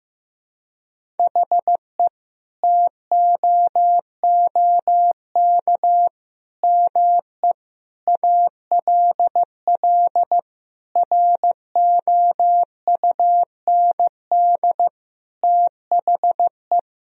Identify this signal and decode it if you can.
Morse code, heard as a keyed tone, about 15 wpm, 700 Hz. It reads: HE TOOK ME ALL ROUND THE
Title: Sentences from Top 1000 Words - Review Part 1 - 15wpm